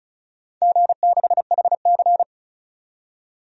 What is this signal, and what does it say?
Morse code, heard as a keyed tone, about 36 wpm, 690 Hz. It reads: G6HC
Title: DXCC-100 Call Signs - 4 Characters - 35wpm